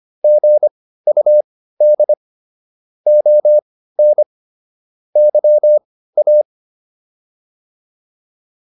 Morse code, heard as a keyed tone, about 25 wpm, spaced as for 15 wpm, 600 Hz. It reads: GUD ON YA